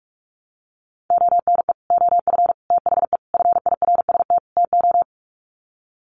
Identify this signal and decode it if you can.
Morse code, heard as a keyed tone, about 44 wpm, 700 Hz. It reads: ONE OF THE FIRST TO
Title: Sets of 5 Words Encore 45wpm